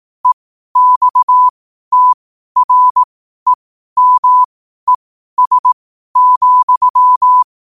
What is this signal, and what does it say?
Morse code, heard as a keyed tone, about 18 wpm, spaced as for 12 wpm, 1000 Hz. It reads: EXTREMES,
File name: qotdfiles / 12wpm